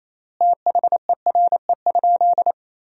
Morse code, heard as a keyed tone, about 28 wpm, 700 Hz. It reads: THERE?
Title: 3x Word-Spacing - Sentences from Top 200 Words - 28wpm